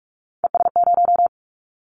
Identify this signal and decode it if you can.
Morse code, heard as a keyed tone, about 46 wpm, 690 Hz. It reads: ES0
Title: Callsign Prefixes for Top 50 DXCC - 45wpm